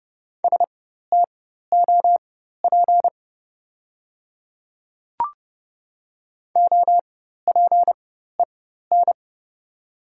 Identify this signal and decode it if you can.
Morse code, heard as a keyed tone, about 30 wpm, spaced as for 14 wpm, 700 Hz. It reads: STOP OPEN
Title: 4x Character Spacing - Top 300 Words - 30wpm